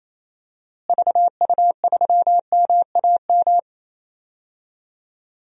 Morse code, heard as a keyed tone, about 28 wpm, 700 Hz. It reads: VU3MAM
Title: DXCC-100 Call Signs - 6 Characters - Repeated 2x - 28wpm